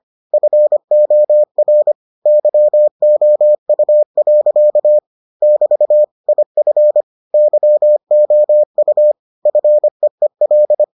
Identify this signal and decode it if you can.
Morse code, heard as a keyed tone, about 25 wpm, 600 Hz. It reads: FOR YOU. <BT> IF YOU FEEL